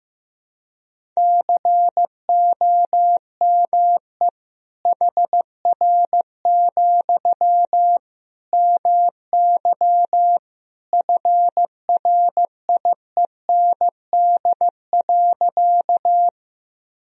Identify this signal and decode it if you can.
Morse code, heard as a keyed tone, about 15 wpm, 700 Hz. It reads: COME HR, MY FRIEND.
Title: Sentences from Top 300 Words and Abbreviations 15wpm